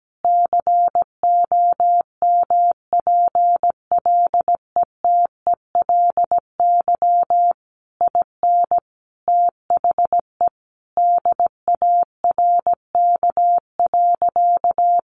Morse code, heard as a keyed tone, about 17 wpm, 700 Hz. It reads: COMPLETELY IN THE DARK.
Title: Sentences from Top 1100 Words 17wpm